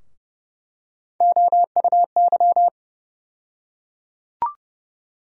Morse code, heard as a keyed tone, about 30 wpm, 700 Hz. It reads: OUY
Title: USA Call Sign Postfixes - 3 Letters 30wpm